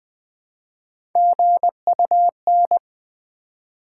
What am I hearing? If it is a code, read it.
Morse code, heard as a keyed tone, about 20 wpm, 700 Hz. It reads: GUN
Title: Build A Word - 1 Letter At A Time - 3-Letter Words - 20wpm